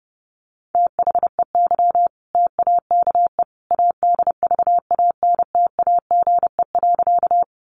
Morse code, heard as a keyed tone, about 30 wpm, 700 Hz. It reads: THEY TAKE ADVANTAGE.
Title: Sentences from Top 900 Words 30wpm